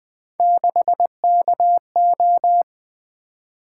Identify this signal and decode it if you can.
Morse code, heard as a keyed tone, about 20 wpm, 700 Hz. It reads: 6KO
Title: Three Letter-Number Combinations - 20wpm